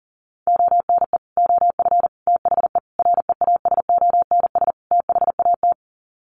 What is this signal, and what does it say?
Morse code, heard as a keyed tone, about 40 wpm, 700 Hz. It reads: ONE OF THE REASONS THAT
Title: Sets of 5 Words Encore 40wpm